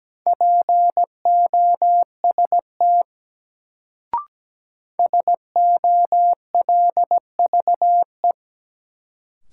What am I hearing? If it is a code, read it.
Morse code, heard as a keyed tone, about 17 wpm, 700 Hz. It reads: POST SOLVE
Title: Sentences from Top 1000 Words - New Word Drill - 17wpm